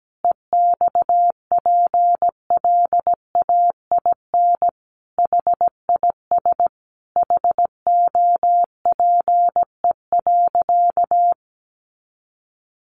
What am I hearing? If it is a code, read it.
Morse code, heard as a keyed tone, about 17 wpm, 700 Hz. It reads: EXPLAIN HIS HOPE.
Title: Sentences from Top 1100 Words 17wpm